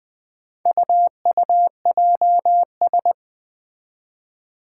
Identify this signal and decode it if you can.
Morse code, heard as a keyed tone, about 20 wpm, 700 Hz. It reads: UUJS
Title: Four Challenging Letter-Number Combinations 20wpm